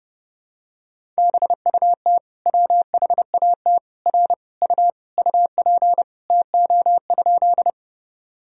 Morse code, heard as a keyed tone, about 30 wpm, 700 Hz. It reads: BUT WHAT R U UP TO?
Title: Sentences from Top 100 Words and Abbreviations 30wpm